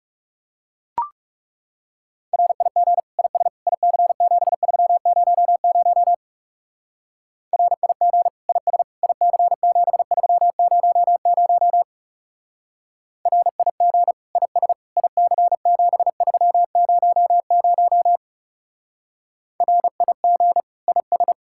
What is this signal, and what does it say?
Morse code, heard as a keyed tone, about 36 wpm, 700 Hz. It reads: RIG IS IC7300 RIG IS IC7300 RIG IS IC7300 RIG IS
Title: Speed-Racing Common QSO Phrases - 30wpm